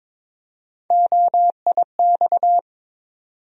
Morse code, heard as a keyed tone, about 22 wpm, 700 Hz. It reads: OIX
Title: USA Call Sign Postfixes - 3 Letters 22wpm